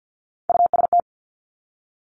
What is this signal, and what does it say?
Morse code, heard as a keyed tone, about 50 wpm, 700 Hz. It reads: UST
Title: Top Three Letter Combinations in Words 50wpm